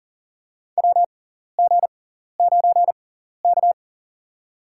Morse code, heard as a keyed tone, about 40 wpm, spaced as for 14 wpm, 700 Hz. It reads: WG9K
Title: USA Call Signs - 4 Characters - 6x Character Spacing - 40wpm